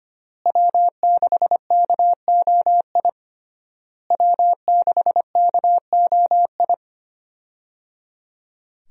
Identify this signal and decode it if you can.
Morse code, heard as a keyed tone, about 25 wpm, 700 Hz. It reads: W6KOI W6KOI
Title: USA Call Signs - 5 Characters - Repeated 3x - 25wpm